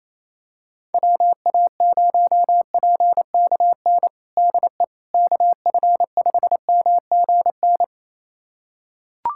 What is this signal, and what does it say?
Morse code, heard as a keyed tone, about 27 wpm, 700 Hz. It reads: WA0PKN DE KF5MGN
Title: USA Callsign DE Callsign - 6 Characters - 28wpm